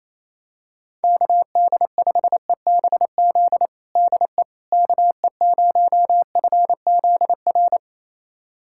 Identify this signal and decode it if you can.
Morse code, heard as a keyed tone, about 28 wpm, 700 Hz. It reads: KD5EBZ DE KE0FZR
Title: USA Callsign DE Callsign - 6 Characters - 28wpm